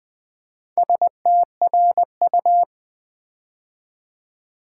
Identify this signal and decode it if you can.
Morse code, heard as a keyed tone, about 20 wpm, 700 Hz. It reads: STRU
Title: Build A Word - 4 Letters At A Time - 8-Letter Words - 20wpm